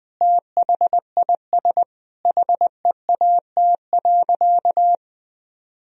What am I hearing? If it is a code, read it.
Morse code, heard as a keyed tone, about 20 wpm, 700 Hz. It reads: THIS HEAT.
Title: Sentences from Top 1000 Words - Review Part 3 - 20wpm